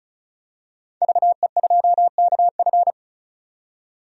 Morse code, heard as a keyed tone, about 36 wpm, 700 Hz. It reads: VE2KF
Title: DXCC-100 Call Signs - 5 Characters - 35wpm